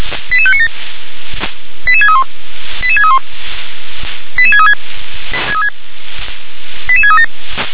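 Five-tone selective call (ZVEI), heard as quick runs of five tones, tones of about 70 ms. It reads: 88597 88631 88631 88636 88637
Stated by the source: Five-tone ZVEI selective call received by SDR.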